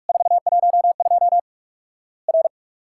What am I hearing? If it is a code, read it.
Morse code, heard as a keyed tone, about 46 wpm, 700 Hz. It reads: 412 R
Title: Worked All Europe DX Contest - QTC Exchange - 45wpm